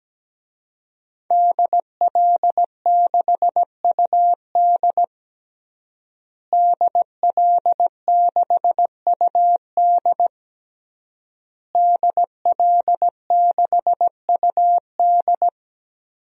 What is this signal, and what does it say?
Morse code, heard as a keyed tone, about 17 wpm, 700 Hz. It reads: DL6UD DL6UD DL6UD
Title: DXCC-100 Call Signs - 5 Characters - Repeated 3x - 17wpm